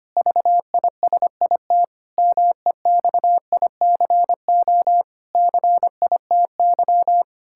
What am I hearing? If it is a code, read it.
Morse code, heard as a keyed tone, about 25 wpm, 700 Hz. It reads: VISIT MEXICO CITY
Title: Pangram Challenge v2 - 25wpm